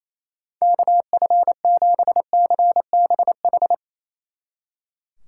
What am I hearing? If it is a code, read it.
Morse code, heard as a keyed tone, about 35 wpm, 700 Hz. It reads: KF7CBH
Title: Speed Racing - USA Call Signs 28wpm